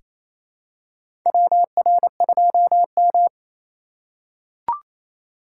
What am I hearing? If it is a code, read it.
Morse code, heard as a keyed tone, about 28 wpm, 700 Hz. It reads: WR2M
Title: USA Call Signs - 4x Character Spacing - 28wpm